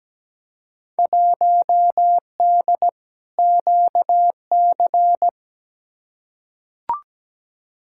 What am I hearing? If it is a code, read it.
Morse code, heard as a keyed tone, about 17 wpm, 700 Hz. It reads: 1D QC
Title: Field Day Exchanges - 17wpm